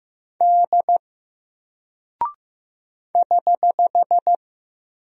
Morse code, heard as a keyed tone, about 15 wpm, 700 Hz. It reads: D <HH>
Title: Single Character (Letter:Number:Punctuation:Prosign) 15wpm